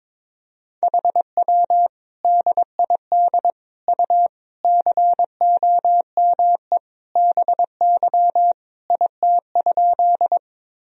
Morse code, heard as a keyed tone, about 22 wpm, 700 Hz. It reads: HW DID U COME BY IT?
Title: Sentences from Top 100 Words and Abbreviations 22wpm